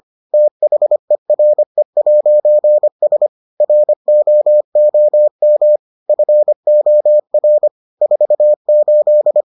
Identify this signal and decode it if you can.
Morse code, heard as a keyed tone, about 25 wpm, 600 Hz. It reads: THERE'S ROOM FOR 48